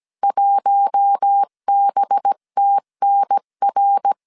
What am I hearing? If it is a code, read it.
Morse code, heard as a keyed tone, about 17 wpm, 800 Hz. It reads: 1BTNR
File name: CW / DET / 1 9 1500C S B 16wpm 2856